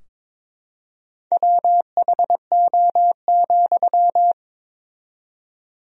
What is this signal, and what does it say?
Morse code, heard as a keyed tone, about 22 wpm, 700 Hz. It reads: WHO,
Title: 4x Word-Spacing - Sentences from Top 100 Words - 22wpm